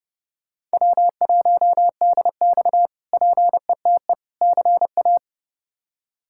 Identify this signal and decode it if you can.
Morse code, heard as a keyed tone, about 30 wpm, 700 Hz. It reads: W1DX PETE CA
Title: K1USN SST QSO Practice - 30wpm